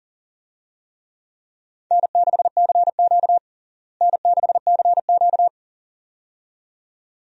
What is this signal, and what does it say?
Morse code, heard as a keyed tone, about 37 wpm, 700 Hz. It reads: N6CQ N6CQ
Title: USA Call Signs - 4 Characters - Repeated 2x - 40wpm